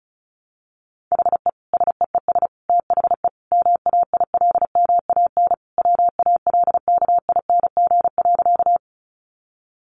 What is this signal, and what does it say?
Morse code, heard as a keyed tone, about 34 wpm, 700 Hz. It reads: HE SEES THE MAILMAN WALKING.
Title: Additional Easy Phrases 35wpm